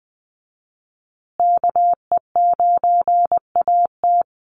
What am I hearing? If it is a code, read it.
Morse code, heard as a keyed tone, about 20 wpm, 700 Hz. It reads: KE9AT
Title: USA Call Signs Repeated 2x 20wpm